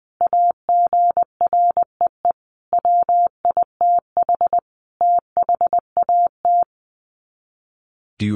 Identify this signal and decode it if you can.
Morse code, heard as a keyed tone, about 20 wpm, 700 Hz. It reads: AGREE WITH THAT
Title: Sets of 5 Words Encore 20wpm